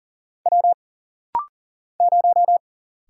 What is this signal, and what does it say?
Morse code, heard as a keyed tone, about 40 wpm, 700 Hz. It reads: W 0
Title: Single Letter-Number - Rapid-Fire 40wpm